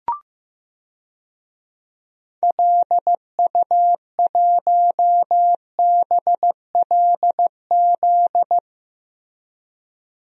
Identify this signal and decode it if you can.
Morse code, heard as a keyed tone, about 15 wpm, 700 Hz. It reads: LU1BLZ
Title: DXCC-100 Call Signs - 6 Characters - Repeated 3x - 15wpm